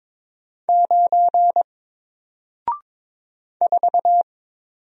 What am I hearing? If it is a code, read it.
Morse code, heard as a keyed tone, about 22 wpm, 700 Hz. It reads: 9 4
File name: Single-Letter-Number-22wpm